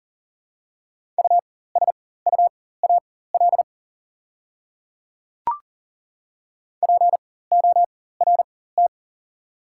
Morse code, heard as a keyed tone, about 38 wpm, spaced as for 18 wpm, 690 Hz. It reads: USUAL PORT
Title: 4x Character Spacing - Top 500 Words - 40wpm